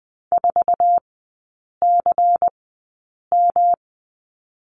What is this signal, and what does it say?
Morse code, heard as a keyed tone, about 20 wpm, spaced as for 8 wpm, 700 Hz. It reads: 4CM
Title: 152 Lesson 25e - Letter p - Three Characters 20wpm